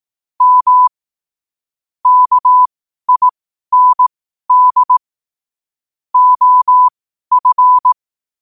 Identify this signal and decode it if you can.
Morse code, heard as a keyed tone, about 18 wpm, spaced as for 12 wpm, 1000 Hz. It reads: M KIND OF
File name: qotdfiles / 12wpm